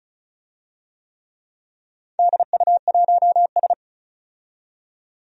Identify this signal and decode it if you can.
Morse code, heard as a keyed tone, about 36 wpm, 690 Hz. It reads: DU1S
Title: DXCC-100 Call Signs - 4 Characters - 35wpm